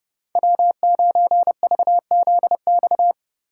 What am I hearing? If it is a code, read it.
Morse code, heard as a keyed tone, about 30 wpm, 700 Hz. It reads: W9VZX
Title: USA Call Signs - 5 Characters - 2x Character Spacing - 30wpm